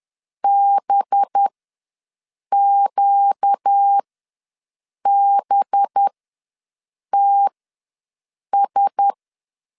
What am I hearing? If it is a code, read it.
Morse code, heard as a keyed tone, about 10 wpm, spaced as for 6 wpm, 800 Hz. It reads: BQBTS
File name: LEZIONI / 5 LBVC / 5 LB ws 10wpm 3616